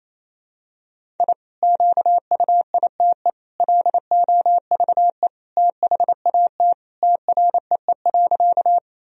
Morse code, heard as a keyed tone, about 28 wpm, 700 Hz. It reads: I QUITE LOVE THAT TREE.
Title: Sentences from Top 1000 Words - Review Part 1 - 28wpm